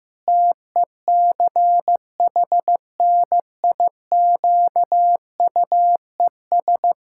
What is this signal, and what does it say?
Morse code, heard as a keyed tone, about 15 wpm, 700 Hz. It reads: TECHNIQUES
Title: Pangram Challenge v2 - 15wpm